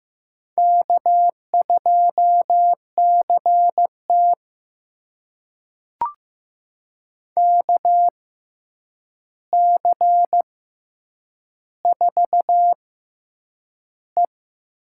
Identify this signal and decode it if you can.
Morse code, heard as a keyed tone, about 15 wpm, spaced as for 8 wpm, 700 Hz. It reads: K2CT KC4E
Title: USA Call Signs - 6x Character Spacing - 15wpm